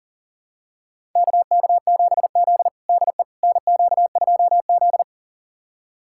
Morse code, heard as a keyed tone, about 40 wpm, 700 Hz. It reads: KK7Z DE NQ2Z
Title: USA Callsign DE Callsign - 4 Characters - 40wpm